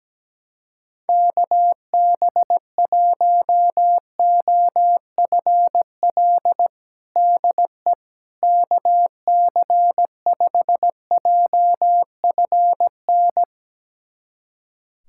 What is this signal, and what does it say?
Morse code, heard as a keyed tone, about 17 wpm, 700 Hz. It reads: KB1OFL DE KC5JFN
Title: USA Callsign DE Callsign - 6 Characters - 17wpm